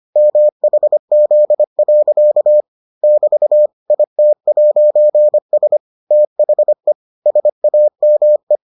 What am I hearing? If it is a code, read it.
Morse code, heard as a keyed tone, about 25 wpm, 600 Hz. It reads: MHZ. <BT> IT'S THE SAME